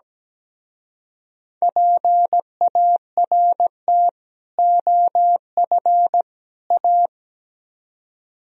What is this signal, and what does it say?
Morse code, heard as a keyed tone, about 20 wpm, 700 Hz. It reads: PART OF A
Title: Speed-Racing - Sets of 3 Words 17wpm